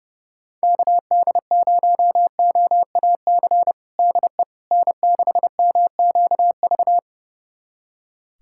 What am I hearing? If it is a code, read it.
Morse code, heard as a keyed tone, about 30 wpm, 700 Hz. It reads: KD0OAC DE N6MQV
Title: USA Callsign DE Callsign 30wpm